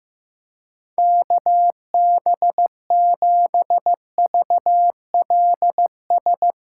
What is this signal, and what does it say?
Morse code, heard as a keyed tone, about 15 wpm, 700 Hz. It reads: KB7VLS
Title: USA Call Signs - 6 Characters - 4x Character Spacing - 15wpm